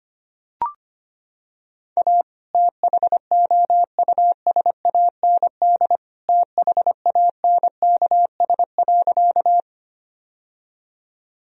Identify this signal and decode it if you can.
Morse code, heard as a keyed tone, about 25 wpm, 700 Hz. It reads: A THOUSAND THANKS.
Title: Sentences from Top 1000 Words - Review Part 3 - 25wpm